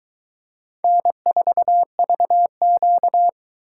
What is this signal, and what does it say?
Morse code, heard as a keyed tone, about 21 wpm, 700 Hz. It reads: N4VQ